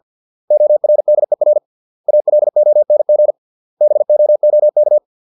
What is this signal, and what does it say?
Morse code, heard as a keyed tone, about 50 wpm, 600 Hz. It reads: ORDER ALONG BOOK